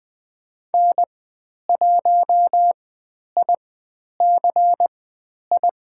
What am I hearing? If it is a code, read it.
Morse code, heard as a keyed tone, about 20 wpm, spaced as for 10 wpm, 700 Hz. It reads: N1IC I
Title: 109 Lesson 18g - Letter m - Call Signs 20wpm